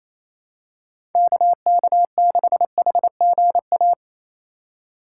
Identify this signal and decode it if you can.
Morse code, heard as a keyed tone, about 28 wpm, 700 Hz. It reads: KK6HGA
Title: USA Call Signs - 6 Characters - 2x Character Spacing - 28wpm